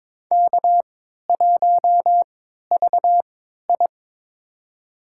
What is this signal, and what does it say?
Morse code, heard as a keyed tone, about 22 wpm, spaced as for 13 wpm, 700 Hz. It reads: K1VI
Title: USA Call Signs - 4 Characters - 3x Character Spacing - 22wpm